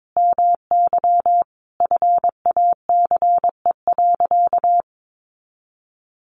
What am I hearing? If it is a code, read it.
Morse code, heard as a keyed tone, about 22 wpm, 700 Hz. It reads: MY FACE.
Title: Sentences from Top 1100 Words 22wpm